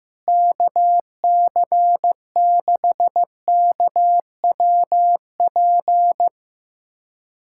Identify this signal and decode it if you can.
Morse code, heard as a keyed tone, about 15 wpm, 700 Hz. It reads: KC6KWP
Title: USA Callsign DE Callsign - 6 Characters - 15wpm